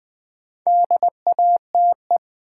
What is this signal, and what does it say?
Morse code, heard as a keyed tone, about 20 wpm, 700 Hz. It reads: DATE
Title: Build A Word - 2 Letters At A Time - 4-Letter Words - 20wpm